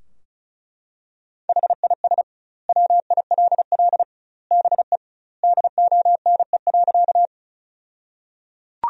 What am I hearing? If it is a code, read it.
Morse code, heard as a keyed tone, about 36 wpm, 700 Hz. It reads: HIS WILL BE DONE.
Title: 2x Word-Spacing - Sentences from Top 500 Words - 35wpm